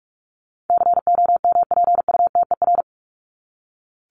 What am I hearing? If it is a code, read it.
Morse code, heard as a keyed tone, about 44 wpm, 700 Hz. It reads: COMPUTER
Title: Top 501-750 Words 45wpm